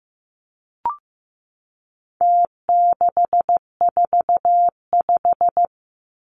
Morse code, heard as a keyed tone, about 15 wpm, 700 Hz. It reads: T645
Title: Numbers - Contest Style - 4-Digit 15wpm